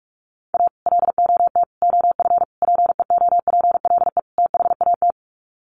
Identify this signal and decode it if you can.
Morse code, heard as a keyed tone, about 44 wpm, 700 Hz. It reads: A LOT OF PEOPLE THAT
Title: Sets of 5 Words Encore2 45wpm